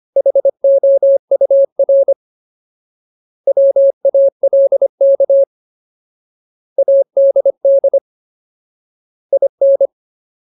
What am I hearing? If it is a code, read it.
Morse code, heard as a keyed tone, about 25 wpm, 550 Hz. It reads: HOUR WALK ADD IN
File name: Ultimate Word List / 25WPM Ultimate